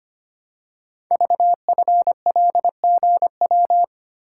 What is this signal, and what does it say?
Morse code, heard as a keyed tone, about 25 wpm, 700 Hz. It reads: VFLGW
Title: Code Groups - 5 Letters - 25wpm